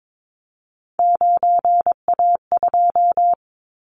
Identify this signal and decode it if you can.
Morse code, heard as a keyed tone, about 22 wpm, 700 Hz. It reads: 9A2
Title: Callsign Prefixes for Top 50 DXCC - 22wpm